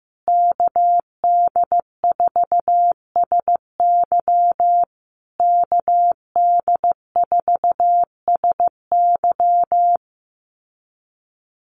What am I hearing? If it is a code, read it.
Morse code, heard as a keyed tone, about 15 wpm, 700 Hz. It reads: KD4SY KD4SY
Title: USA Call Signs Repeated 3x 15wpm